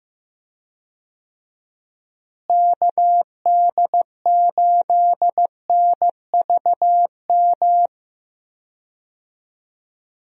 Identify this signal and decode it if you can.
Morse code, heard as a keyed tone, about 15 wpm, 700 Hz. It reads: KD8NVM
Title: USA Call Signs - 6 Characters - Repeated 2x - 15wpm